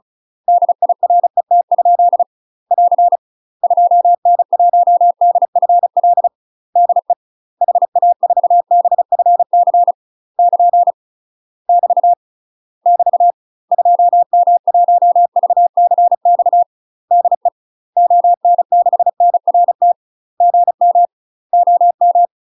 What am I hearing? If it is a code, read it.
Morse code, heard as a keyed tone, about 36 wpm, 700 Hz. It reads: DIRET? <AR> 2N1DFL DE HA4BFC <KN> <BT> <BT> 2M1VCX DE ON6NRT GM OM
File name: UK MASTER QSOs TRAINING / 35WPM / UK MASTER QSOs 0010